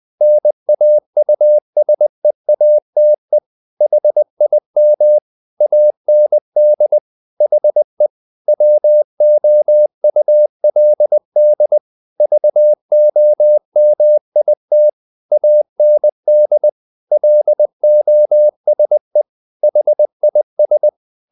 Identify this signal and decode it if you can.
Morse code, heard as a keyed tone, about 20 wpm, 600 Hz. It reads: NAUSEATE HIM AND HE WOULD VOMIT AND LOSE HIS
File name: the old man and the sea part 2 20wpm